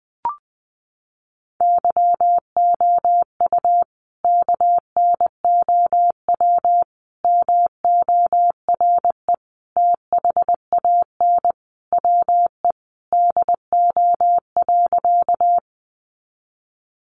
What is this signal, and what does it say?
Morse code, heard as a keyed tone, about 19 wpm, 700 Hz. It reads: YOU KNOW MORE THAN WE DO.
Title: Sentences from Top 100 Words 20wpm